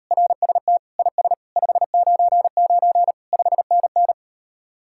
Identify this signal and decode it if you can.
Morse code, heard as a keyed tone, about 36 wpm, 700 Hz. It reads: RST IS 599 5NN